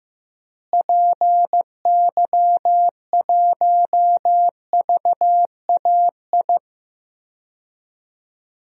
Morse code, heard as a keyed tone, about 15 wpm, 700 Hz. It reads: PY1VAI
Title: DXCC-100 Call Signs - 6 Characters - Repeated 2x - 15wpm